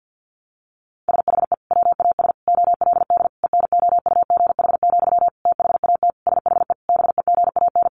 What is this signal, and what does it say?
Morse code, heard as a keyed tone, about 50 wpm, 700 Hz. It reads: SHE WAS OLD ENOUGH, THAT SHE BEGAN
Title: Sentences from Top 800 Words 50wpm